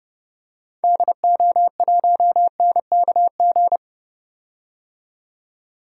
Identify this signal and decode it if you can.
Morse code, heard as a keyed tone, about 30 wpm, 700 Hz. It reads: DO1NKG
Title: DXCC-100 Call Signs - 6 Characters - 30wpm